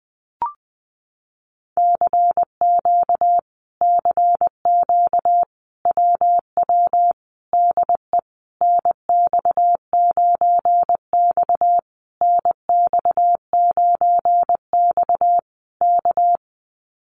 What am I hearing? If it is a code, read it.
Morse code, heard as a keyed tone, about 20 wpm, 700 Hz. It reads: CQ CQ WW DE NX9X NX9X K
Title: Common QSO Phrases Encore 20wpm